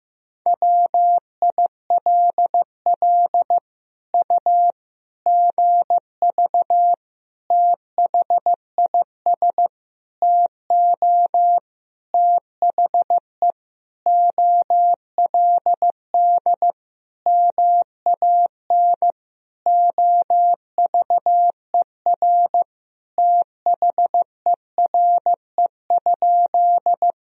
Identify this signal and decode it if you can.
Morse code, heard as a keyed tone, about 15 wpm, 700 Hz. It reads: WILL U GV THIS TO THE OLD MAN OVER THERE?
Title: Sentences from Top 200 Words and Abbreviations 15wpm